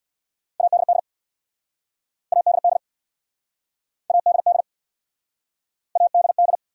Morse code, heard as a keyed tone, about 68 wpm, 700 Hz. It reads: ADD ADD ADD ADD